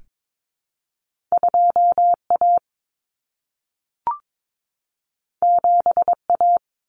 Morse code, heard as a keyed tone, about 22 wpm, 700 Hz. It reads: 2A 7A
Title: Common Cut Numbers - 2-Digit 22wpm